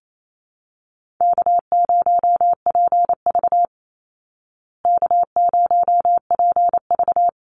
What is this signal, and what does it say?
Morse code, heard as a keyed tone, about 28 wpm, 700 Hz. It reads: K0PV K0PV
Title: Easy Call Signs Repeated 3x - 28wpm